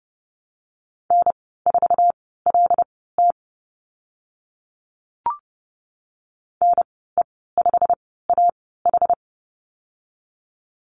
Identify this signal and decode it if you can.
Morse code, heard as a keyed tone, about 31 wpm, spaced as for 17 wpm, 700 Hz. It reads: N4LT NE5AH
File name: Lesson 15g - Number 5 - Call Signs 30wpm